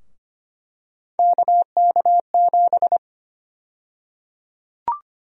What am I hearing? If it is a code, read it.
Morse code, heard as a keyed tone, about 25 wpm, 700 Hz. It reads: KK7
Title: USA Call Sign Prefixes - Repeated 2x - 25wpm